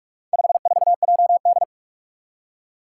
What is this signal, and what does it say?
Morse code, heard as a keyed tone, about 46 wpm, 700 Hz. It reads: 54JD
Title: Four Challenging Letter-Number Combinations 45wpm